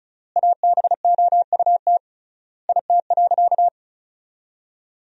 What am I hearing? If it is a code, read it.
Morse code, heard as a keyed tone, about 36 wpm, 700 Hz. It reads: ABOUT IT.
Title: 3x Word-Spacing - Sentences from Top 500 Words - 35wpm